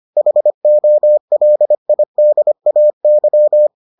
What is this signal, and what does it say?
Morse code, heard as a keyed tone, about 25 wpm, 600 Hz. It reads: HOLIDAY